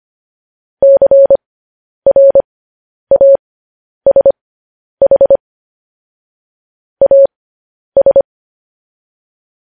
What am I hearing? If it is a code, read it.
Morse code, heard as a keyed tone, about 25 wpm, spaced as for 10 wpm, 550 Hz. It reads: CRASH AS
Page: news headlines in morse code at 10 wpm